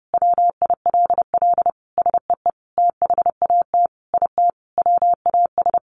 Morse code, heard as a keyed tone, about 30 wpm, 700 Hz. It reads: WILL SEE THAT IT WAS
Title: Sentences from Top 800 Words 30wpm